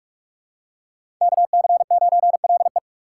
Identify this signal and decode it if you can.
Morse code, heard as a keyed tone, about 47 wpm, 700 Hz. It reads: KC9LE
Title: USA Call Signs - 5 Characters - 3x Character Spacing - 45wpm